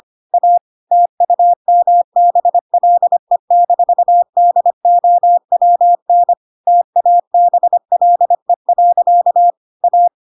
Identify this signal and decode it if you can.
Morse code, heard as a keyed tone, about 25 wpm, 700 Hz. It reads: A TUMBLE-DOWN TABLE. A
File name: Pinocchio / Pinocchio ch3 25 25 ik1hge